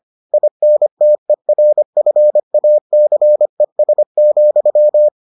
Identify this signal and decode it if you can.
Morse code, heard as a keyed tone, about 25 wpm, 600 Hz. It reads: INTERFACES,